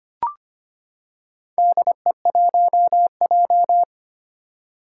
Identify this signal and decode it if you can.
Morse code, heard as a keyed tone, about 25 wpm, 700 Hz. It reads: DE1J
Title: Four Challenging Letter-Number Combinations 25wpm